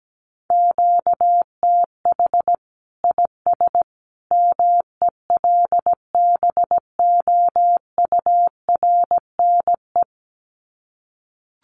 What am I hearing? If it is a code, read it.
Morse code, heard as a keyed tone, about 17 wpm, 700 Hz. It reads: QTH IS MELBOURNE
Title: Common QSO Phrases 17wpm